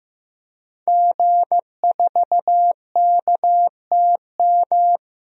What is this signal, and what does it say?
Morse code, heard as a keyed tone, about 15 wpm, 700 Hz. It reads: G4KTM
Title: DXCC-100 Call Signs - 5 Characters - Repeated 3x - 15wpm